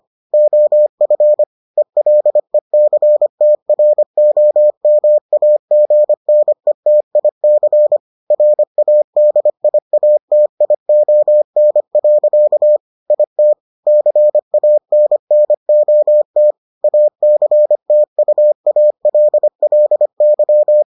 Morse code, heard as a keyed tone, about 25 wpm, 600 Hz. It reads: OF ELECTROMAGNETIC RADIATION. IT CANNOT ACTUALLY